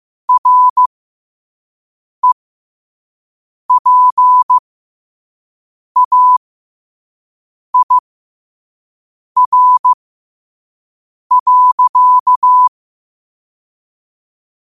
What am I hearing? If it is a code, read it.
Morse code, heard as a keyed tone, about 15 wpm, spaced as for 5 wpm, 1000 Hz. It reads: REPAIR.
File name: qotdfiles / 5wpm